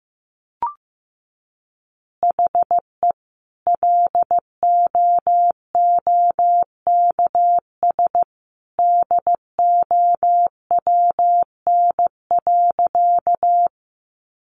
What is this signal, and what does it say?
Morse code, heard as a keyed tone, about 15 wpm, 700 Hz. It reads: HE LOOKS DOWN.
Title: Additional Easy Phrases 15wpm